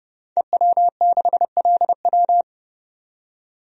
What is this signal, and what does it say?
Morse code, heard as a keyed tone, about 30 wpm, 700 Hz. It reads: EW6LW
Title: DXCC-100 Call Signs - 5 Characters - 30wpm